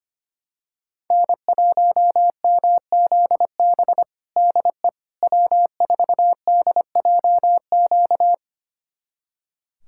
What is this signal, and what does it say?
Morse code, heard as a keyed tone, about 25 wpm, 700 Hz. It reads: N1MZB DE W4DJQ
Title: USA Callsign DE Callsign - 5 Characters - 25wpm